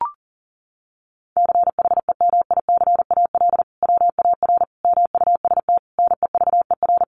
Morse code, heard as a keyed tone, about 40 wpm, 700 Hz. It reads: CHEMICAL WAR MUST NEVER
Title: Sentences from Top 1200 Words 40wpm